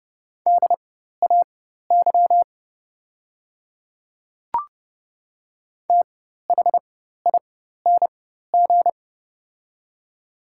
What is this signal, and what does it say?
Morse code, heard as a keyed tone, about 30 wpm, spaced as for 14 wpm, 700 Hz. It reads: DAY THING
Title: 4x Character Spacing - Top 200 Words - 30wpm